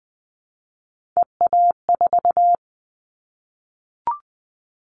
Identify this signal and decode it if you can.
Morse code, heard as a keyed tone, about 20 wpm, 700 Hz. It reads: EA4
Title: Callsign Prefixes for Top 25 DXCC - 20wpm